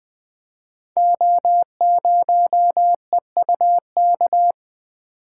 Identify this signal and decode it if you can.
Morse code, heard as a keyed tone, about 20 wpm, 700 Hz. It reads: O0EUK
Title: Code Groups - 5 Letter:Numbers - 20wpm